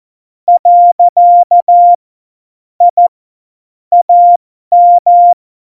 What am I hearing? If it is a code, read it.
Morse code, heard as a keyed tone, about 14 wpm, spaced as for 12 wpm, 700 Hz. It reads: . I AM
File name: Morsekurs / Witze